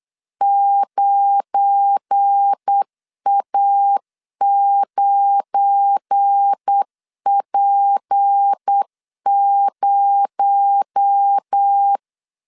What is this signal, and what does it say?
Morse code, heard as a keyed tone, about 8 wpm, 800 Hz. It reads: 9A9P0
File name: LEZIONI / 7 GF9Z / 7 GF9 8wpm 3545